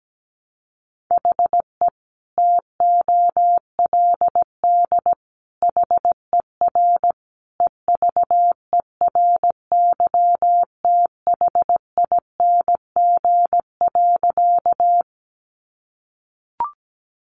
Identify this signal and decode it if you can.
Morse code, heard as a keyed tone, about 17 wpm, 700 Hz. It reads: HE TOLD HER EVERYTHING.
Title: Sentences from Top 700 Words 17wpm